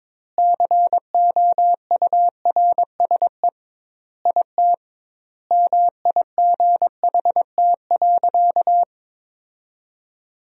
Morse code, heard as a keyed tone, about 22 wpm, 700 Hz. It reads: COURSE IT MIGHT.
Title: 2x Word-Spacing - Sentences from Top 500 Words - 22wpm